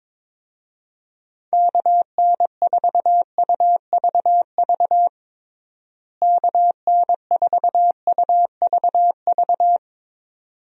Morse code, heard as a keyed tone, about 22 wpm, 700 Hz. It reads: KN4UVV KN4UVV
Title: USA Call Signs - 6 Characters - Repeated 2x - 22wpm